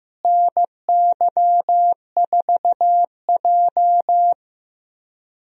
Morse code, heard as a keyed tone, about 15 wpm, 700 Hz. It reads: NY4J
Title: USA Call Signs - 4 Characters - Repeated 3x - 15wpm